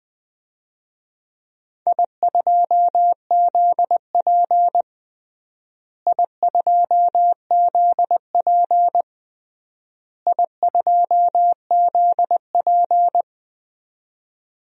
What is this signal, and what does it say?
Morse code, heard as a keyed tone, about 20 wpm, 700 Hz. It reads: I2ZP I2ZP I2ZP
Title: DXCC-100 Call Signs - 4 Characters - Repeated 3x - 20wpm